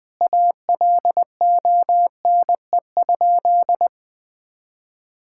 Morse code, heard as a keyed tone, about 21 wpm, 700 Hz. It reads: ALONE?
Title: Sentences from Top 1600 Words 20wpm